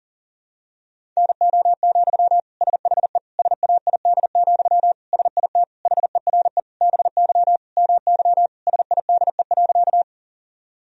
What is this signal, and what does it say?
Morse code, heard as a keyed tone, about 40 wpm, 700 Hz. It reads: NO, SHE SAID, SIT HERE BY MY SIDE.
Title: Sentences from Top 1000 Words - Review Part 1 - 40wpm